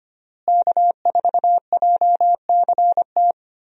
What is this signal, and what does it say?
Morse code, heard as a keyed tone, about 25 wpm, 700 Hz. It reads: K4JCT
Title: USA Call Signs - 5 Characters - Repeated 3x - 25wpm